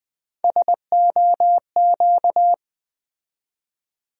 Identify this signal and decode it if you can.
Morse code, heard as a keyed tone, about 20 wpm, 700 Hz. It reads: SOQ
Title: USA Call Sign Postfixes - 3 Letters 20wpm